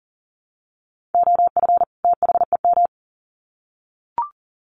Morse code, heard as a keyed tone, about 40 wpm, 700 Hz. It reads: OF THEM
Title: Sets of 2 Words 40wpm